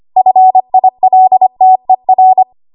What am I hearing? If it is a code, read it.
Morse code, heard as a keyed tone, about 25 wpm, 750 Hz. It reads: FILTER
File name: Archive / 25 WPM / 25WPM